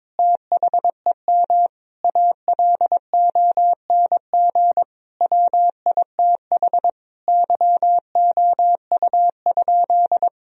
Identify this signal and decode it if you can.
Morse code, heard as a keyed tone, about 22 wpm, 700 Hz. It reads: THEM ALONG WITH YOU?
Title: Sentences from Top 1000 Words - Review Part 3 - 22wpm